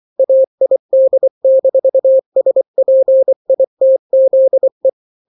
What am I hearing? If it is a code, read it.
Morse code, heard as a keyed tone, about 24 wpm, 530 Hz. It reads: AID-SPITZE
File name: daily / 23wpm